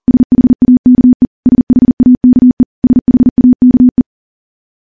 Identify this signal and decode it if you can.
Morse code, heard as a keyed tone, about 40 wpm, 250 Hz. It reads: SHAKE SHAKE SHAKE
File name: morse / morse-training-40WPM